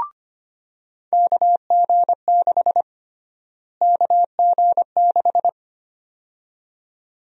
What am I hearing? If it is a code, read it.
Morse code, heard as a keyed tone, about 25 wpm, 700 Hz. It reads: KG6 KG6
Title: USA Call Sign Prefixes - Repeated 2x - 25wpm